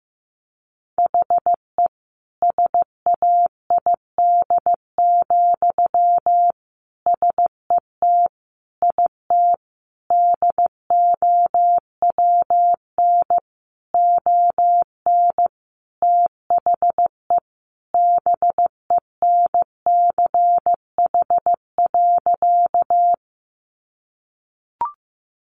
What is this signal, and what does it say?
Morse code, heard as a keyed tone, about 15 wpm, 700 Hz. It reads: HE SAID, SET IT DOWN ON THE BENCH.
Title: Sentences from Top 1400 Words 15wpm